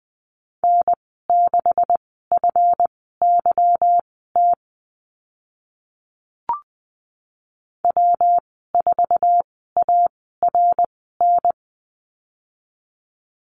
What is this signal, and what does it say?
Morse code, heard as a keyed tone, about 20 wpm, spaced as for 14 wpm, 700 Hz. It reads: N6FYT W4ARN
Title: 147 Lesson 24g - Letter y - Call Signs 20wpm